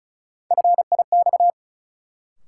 Morse code, heard as a keyed tone, about 36 wpm, 700 Hz. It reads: FIX
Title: Sentences from Top 1400 Words - New Word Drill - 35wpm